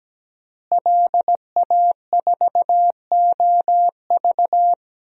Morse code, heard as a keyed tone, about 17 wpm, 700 Hz. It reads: LA4OV
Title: DXCC-100 Call Signs - 5 Characters - Repeated 3x - 17wpm